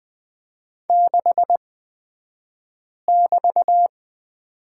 Morse code, heard as a keyed tone, about 20 wpm, spaced as for 5 wpm, 700 Hz. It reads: 6<BT>
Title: 223 Lesson 36d - BT - Two Characters 20wpm